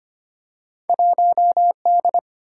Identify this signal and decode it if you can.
Morse code, heard as a keyed tone, about 25 wpm, 700 Hz. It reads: 1D
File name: Two Letter-Number Combinations - 25wpm